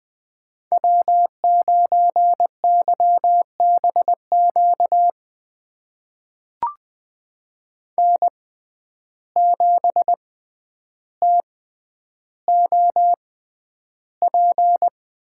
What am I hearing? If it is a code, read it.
Morse code, heard as a keyed tone, about 20 wpm, spaced as for 10 wpm, 700 Hz. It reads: W9YBQ N7TOP
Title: USA Call Signs - 5 Characters - 6x Character Spacing - 20wpm